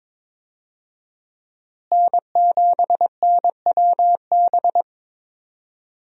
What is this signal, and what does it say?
Morse code, heard as a keyed tone, about 21 wpm, 700 Hz. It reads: N7NWB
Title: USA Call Signs - 5 Characters - Repeated 2x - 22wpm